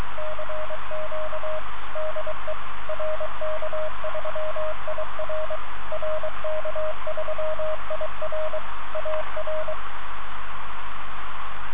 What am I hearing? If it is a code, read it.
Morse code, heard as a keyed tone, about 23 wpm, 620 Hz, with radio noise. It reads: CQ DE RK3IR RK3IR AR